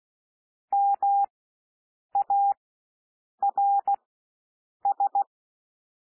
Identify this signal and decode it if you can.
Morse code, heard as a keyed tone, about 16 wpm, spaced as for 7 wpm, 800 Hz. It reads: MARS
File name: cwmp3 / W1 SOUND WORDS 16 wpm